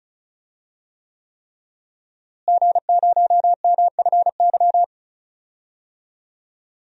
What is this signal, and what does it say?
Morse code, heard as a keyed tone, about 34 wpm, 700 Hz. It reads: G0MFY
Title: DXCC-100 Call Signs - 5 Characters - Repeated 3x - 35wpm